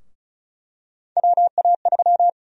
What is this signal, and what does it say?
Morse code, heard as a keyed tone, about 36 wpm, 700 Hz. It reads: WA3
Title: USA Call Sign Prefixes - 35wpm